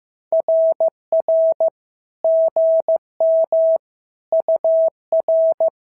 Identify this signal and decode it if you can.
Morse code, heard as a keyed tone, about 14 wpm, 650 Hz. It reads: RR GM UR
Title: SOTA QSO Practice 15wpm